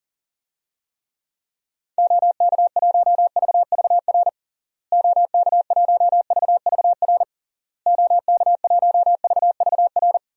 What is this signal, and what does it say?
Morse code, heard as a keyed tone, about 37 wpm, 700 Hz. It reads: OK1VVR OK1VVR OK1VVR
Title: DXCC-100 Call Signs - 6 Characters - Repeated 3x - 40wpm